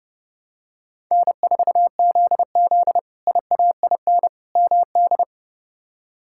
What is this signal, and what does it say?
Morse code, heard as a keyed tone, about 32 wpm, 700 Hz. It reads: N4ZZ IAIN MD
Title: K1USN SST QSO Practice - 30wpm